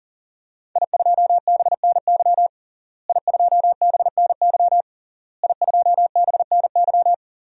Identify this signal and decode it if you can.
Morse code, heard as a keyed tone, about 40 wpm, 700 Hz. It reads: I2BNY I2BNY I2BNY
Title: DXCC-100 Call Signs - 5 Characters - Repeated 3x - 40wpm